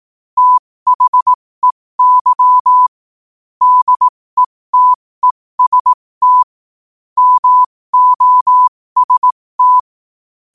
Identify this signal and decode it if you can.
Morse code, heard as a keyed tone, about 18 wpm, spaced as for 15 wpm, 1000 Hz. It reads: THEY DETEST MOST
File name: qotdfiles / 15wpm